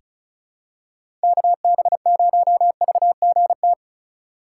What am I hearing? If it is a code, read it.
Morse code, heard as a keyed tone, about 36 wpm, 700 Hz. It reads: KB0VGT
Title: USA Call Signs - 4x Character Spacing - 35wpm